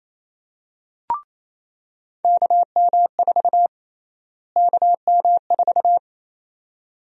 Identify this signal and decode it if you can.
Morse code, heard as a keyed tone, about 28 wpm, 700 Hz. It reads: KM4 KM4
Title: USA Call Sign Prefixes - Repeated 2x - 28wpm